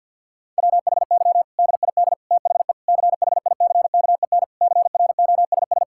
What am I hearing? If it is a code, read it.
Morse code, heard as a keyed tone, about 52 wpm, 700 Hz. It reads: WHY DID THE CHICKEN CROSS
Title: Sentences from Top 1000 Words - Review Part 5 - 50wpm